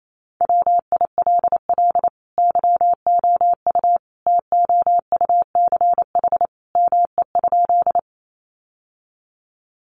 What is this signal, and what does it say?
Morse code, heard as a keyed tone, about 28 wpm, 700 Hz. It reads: WILL YOU TOUCH ME?
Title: Sentences from Top 1200 Words 28wpm